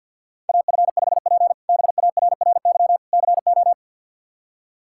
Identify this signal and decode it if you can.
Morse code, heard as a keyed tone, about 46 wpm, 700 Hz. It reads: AF5P BARRY CO
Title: K1USN SST QSO Practice - 50wpm